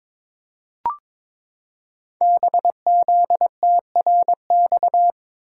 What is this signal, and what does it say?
Morse code, heard as a keyed tone, about 22 wpm, 700 Hz. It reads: BZTRX
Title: Code Groups - 5 Letter:Numbers - 22wpm